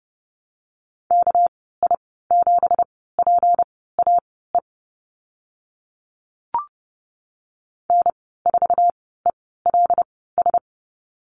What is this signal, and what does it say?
Morse code, heard as a keyed tone, about 30 wpm, spaced as for 17 wpm, 700 Hz. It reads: KI7PAE N4ELS
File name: Lesson 37g - Letter x - Call Signs 30wpm